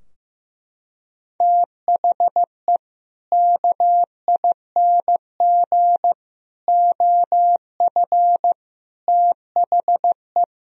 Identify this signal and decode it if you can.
Morse code, heard as a keyed tone, about 15 wpm, 700 Hz. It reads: THE KING OF THE
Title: Sentences from Top 1000 Words - Review Part 2 - 15wpm